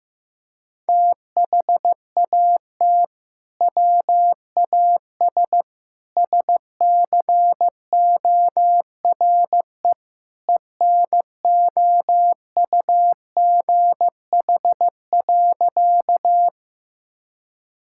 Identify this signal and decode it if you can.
Morse code, heard as a keyed tone, about 16 wpm, 700 Hz. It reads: THAT WAS SCORE ENOUGH.
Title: Sentences from Top 1600 Words 15wpm